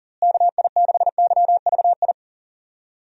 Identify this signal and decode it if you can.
Morse code, heard as a keyed tone, about 40 wpm, 700 Hz. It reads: KI6YVI
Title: USA Call Signs - 6 Characters - Repeated 2x - 40wpm